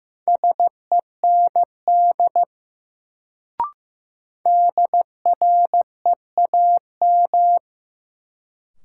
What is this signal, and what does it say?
Morse code, heard as a keyed tone, about 15 wpm, 700 Hz. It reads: SEND DREAM
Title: Sentences from Top 1500 Words - New Word Drill - 15wpm